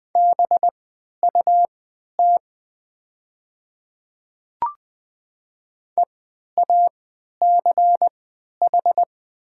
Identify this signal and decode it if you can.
Morse code, heard as a keyed tone, about 20 wpm, spaced as for 11 wpm, 700 Hz. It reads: BUT EACH
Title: 3x Character Spacing - Top 200 Words - 20wpm